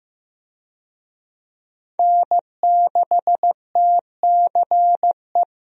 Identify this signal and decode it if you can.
Morse code, heard as a keyed tone, about 15 wpm, 700 Hz. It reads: N6TCE
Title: USA Call Signs - 5 Characters - Repeated 3x - 15wpm